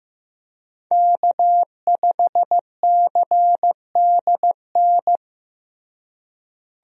Morse code, heard as a keyed tone, about 15 wpm, 700 Hz. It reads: K5CDN
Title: USA Call Signs - 5 Characters - Repeated 3x - 15wpm